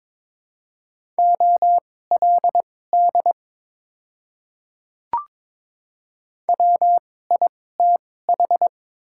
Morse code, heard as a keyed tone, about 23 wpm, spaced as for 16 wpm, 700 Hz. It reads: OLD WITH
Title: 2x Character Spacing - Top 300 Words - 22wpm